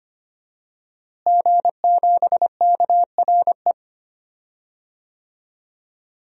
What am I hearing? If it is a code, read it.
Morse code, heard as a keyed tone, about 25 wpm, 700 Hz. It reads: G7KRE
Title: DXCC-100 Call Signs - 5 Characters - Repeated 3x - 25wpm